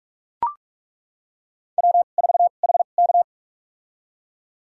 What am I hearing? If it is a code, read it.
Morse code, heard as a keyed tone, about 42 wpm, spaced as for 31 wpm, 700 Hz. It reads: W4HK
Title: USA Call Signs - 4 Characters - 2x Character Spacing - 45wpm